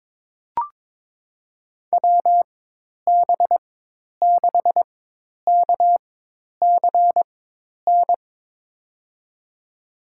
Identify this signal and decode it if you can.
Morse code, heard as a keyed tone, about 21 wpm, spaced as for 10 wpm, 700 Hz. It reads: WB6KCN
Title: USA Call Signs - 6 Characters - 4x Character Spacing - 22wpm